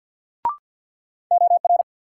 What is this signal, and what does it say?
Morse code, heard as a keyed tone, about 54 wpm, 700 Hz. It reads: OR
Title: Sentences from Top 100 Words - New Word Drill - 50wpm